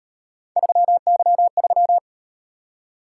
Morse code, heard as a keyed tone, about 38 wpm, 700 Hz. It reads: 3Y3